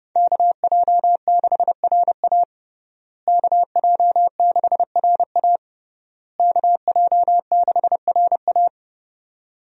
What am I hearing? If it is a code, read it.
Morse code, heard as a keyed tone, about 32 wpm, 700 Hz. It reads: KJ6RA KJ6RA KJ6RA
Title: USA Call Signs - 5 Characters - Repeated 3x - 30wpm